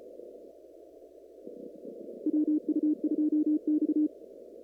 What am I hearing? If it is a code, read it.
Morse code, heard as a keyed tone, about 34 wpm, 300 Hz. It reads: WU2X
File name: CW / kt5j wpx cw-op-k5pi